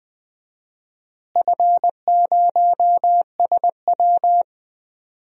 Morse code, heard as a keyed tone, about 20 wpm, 700 Hz. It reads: F0SW
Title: DXCC-100 Call Signs - 4 Characters - Repeated 3x - 20wpm